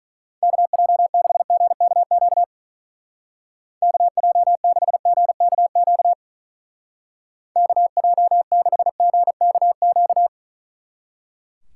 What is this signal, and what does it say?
Morse code, heard as a keyed tone, about 40 wpm, 700 Hz. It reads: KJ6GKQ KJ6GKQ KJ6GKQ
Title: Speed Racing - USA 2x3 Call Signs 35wpm